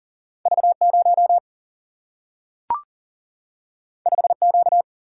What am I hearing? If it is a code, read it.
Morse code, heard as a keyed tone, about 39 wpm, 700 Hz. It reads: V0 5Q
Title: Single-Letter Single-Number 40wpm